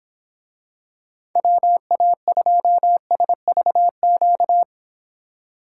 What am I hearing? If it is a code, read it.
Morse code, heard as a keyed tone, about 25 wpm, 700 Hz. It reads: WA2SVQ